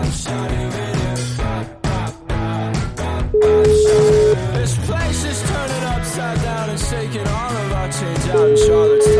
Ringback tone (Europe, 425 Hz), with noise.